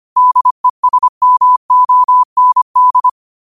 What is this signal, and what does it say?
Morse code, heard as a keyed tone, about 25 wpm, 1000 Hz. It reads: DESMOND